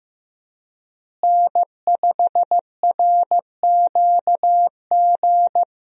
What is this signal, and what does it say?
Morse code, heard as a keyed tone, about 15 wpm, 700 Hz. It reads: N5RQG
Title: USA Call Signs - 5 Characters - Repeated 3x - 15wpm